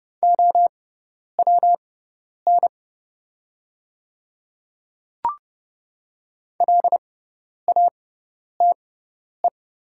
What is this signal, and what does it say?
Morse code, heard as a keyed tone, about 30 wpm, spaced as for 10 wpm, 700 Hz. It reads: OWN LATE
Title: 6x Character Spacing - Top 300 Words - 30wpm